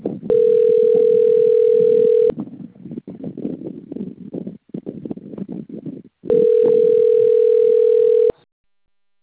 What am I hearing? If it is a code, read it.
Ringback tone (North America), 440 + 480 Hz.